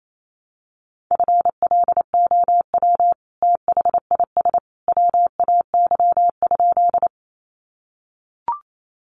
Morse code, heard as a keyed tone, about 28 wpm, 700 Hz. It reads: FLOW THIS WAY?
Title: Sentences from Top 1200 Words 28wpm